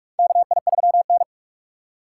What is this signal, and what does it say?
Morse code, heard as a keyed tone, about 46 wpm, 700 Hz. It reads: KI3N
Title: USA Call Signs - 4x Character Spacing - 45wpm